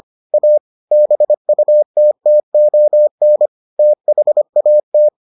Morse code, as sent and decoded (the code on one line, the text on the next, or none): .- / -... ..- - - --- -. / - .... .- -
A BUTTON THAT